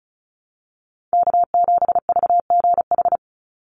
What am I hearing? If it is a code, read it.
Morse code, heard as a keyed tone, about 34 wpm, 700 Hz. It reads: K7VGH
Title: USA Call Signs 35wpm